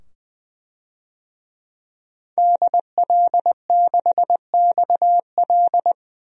Morse code, heard as a keyed tone, about 20 wpm, 700 Hz. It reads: DL6XL
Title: DXCC-100 Call Signs - 5 Characters - 20wpm